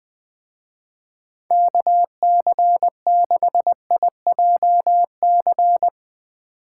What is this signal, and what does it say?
Morse code, heard as a keyed tone, about 22 wpm, 700 Hz. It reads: KC6IJC
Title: Speed Racing - USA Call Signs 20wpm